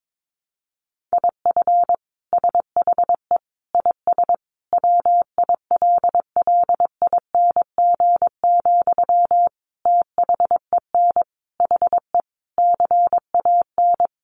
Morse code, heard as a keyed tone, about 22 wpm, 700 Hz. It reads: IF SHE IS WILLING, THEN HE CAN
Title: Sentences from Top 1000 Words 22wpm